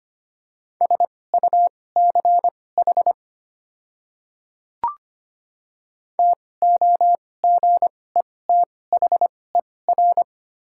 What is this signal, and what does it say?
Morse code, heard as a keyed tone, about 25 wpm, spaced as for 18 wpm, 700 Hz. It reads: SUCH TOGETHER
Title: 2x Character Spacing - Top 300 Words - 25wpm